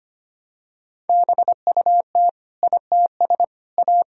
Morse code, heard as a keyed tone, about 25 wpm, 700 Hz. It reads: BUT ITS A
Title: Sentences from Top 1000 Words - Review Part 2 - 25wpm